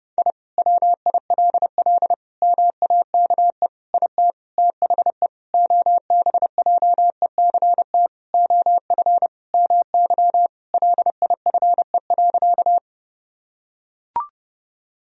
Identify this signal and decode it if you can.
Morse code, heard as a keyed tone, about 30 wpm, 700 Hz. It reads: I WILL MAKE IT THE OBJECT OF MY LIFE.
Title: Sentences from Top 1000 Words - Review Part 3 - 30wpm